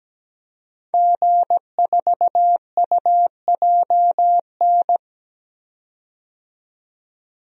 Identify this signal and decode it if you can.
Morse code, heard as a keyed tone, about 17 wpm, 700 Hz. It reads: G4UJN
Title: DXCC-100 Call Signs - 5 Characters - 17wpm